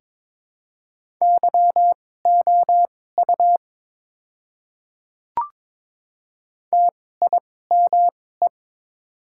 Morse code, heard as a keyed tone, about 22 wpm, spaced as for 16 wpm, 690 Hz. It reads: YOU TIME
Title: 2x Character Spacing - Top 100 Words - 22wpm